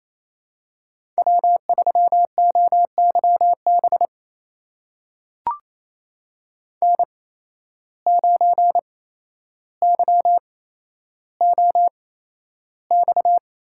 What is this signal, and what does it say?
Morse code, heard as a keyed tone, about 28 wpm, spaced as for 12 wpm, 700 Hz. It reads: W3OYB N9YOX
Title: USA Call Signs - 5 Characters - 8x Character Spacing - 28wpm